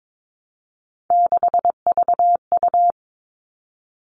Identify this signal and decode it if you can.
Morse code, heard as a keyed tone, about 22 wpm, 700 Hz. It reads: 6VU
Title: All Cut Numbers - 3-Digit 22wpm